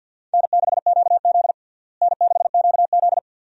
Morse code, heard as a keyed tone, about 52 wpm, 700 Hz. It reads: N6QZ N6QZ
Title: USA Call Signs - 4 Characters - Repeated 2x - 50wpm